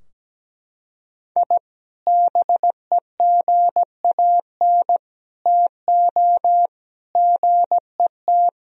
Morse code, heard as a keyed tone, about 17 wpm, 700 Hz. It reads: I BEGAN TO GET
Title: Sentences from Top 1000 Words - Review Part 4 - 17wpm